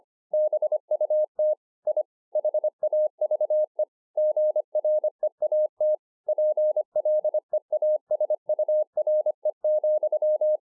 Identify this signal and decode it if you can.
Morse code, heard as a keyed tone, about 26 wpm, 610 Hz. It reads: BUT I HAVE GREAT PLEASURE,
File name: 2011-04-mp3 / BEGINNERS DIARY 25WPM